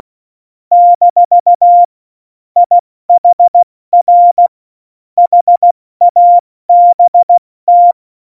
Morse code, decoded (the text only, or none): - IHR HABT